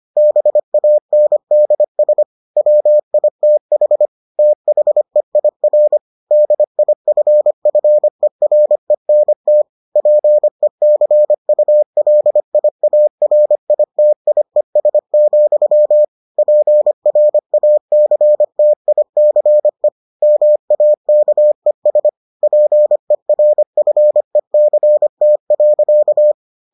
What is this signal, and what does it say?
Morse code, heard as a keyed tone, about 25 wpm, 600 Hz. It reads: BANDS WITH THEIR DIFFERENT PECULIARITIES, PRACTICE MAKES PERFECT.